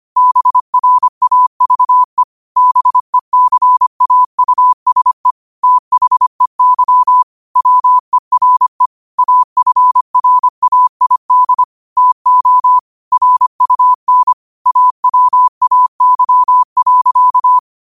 Morse code, heard as a keyed tone, about 25 wpm, 1000 Hz. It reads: BRAVE BECAUSE THEY WERE AFRAID TO RUN AWAY.
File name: qotdfiles / 25wpm